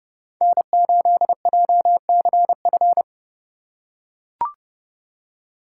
Morse code, heard as a keyed tone, about 29 wpm, 700 Hz. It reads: N8JCF